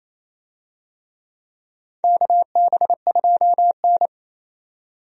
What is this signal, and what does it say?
Morse code, heard as a keyed tone, about 27 wpm, 700 Hz. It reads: KB2N
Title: USA Call Signs - 4 Characters - 28wpm